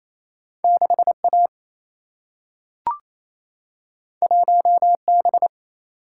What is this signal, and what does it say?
Morse code, heard as a keyed tone, about 28 wpm, 700 Hz. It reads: 6A 1B
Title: Single-Letter Single-Number 28wpm